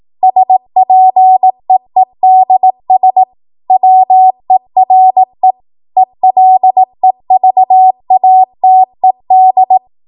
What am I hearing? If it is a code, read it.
Morse code, heard as a keyed tone, about 18 wpm, 750 Hz. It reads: SPEEDS WERE ELEVATED
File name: Morse / 18WPM